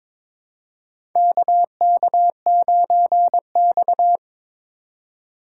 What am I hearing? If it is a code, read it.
Morse code, heard as a keyed tone, about 23 wpm, 700 Hz. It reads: KK9X
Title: USA Call Signs - 4 Characters - 2x Character Spacing - 22wpm